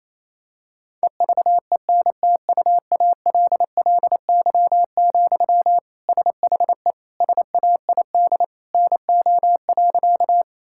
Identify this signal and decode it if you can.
Morse code, heard as a keyed tone, about 28 wpm, 700 Hz. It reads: EVENTUALLY, SHE SAID NO.
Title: Sentences from Top 1000 Words - Review Part 6 - 28wpm